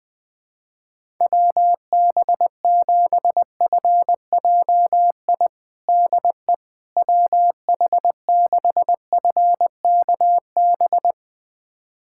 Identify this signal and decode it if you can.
Morse code, heard as a keyed tone, about 20 wpm, 700 Hz. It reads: WB7FJI DE WH6FKB
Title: USA Callsign DE Callsign - 6 Characters - 20wpm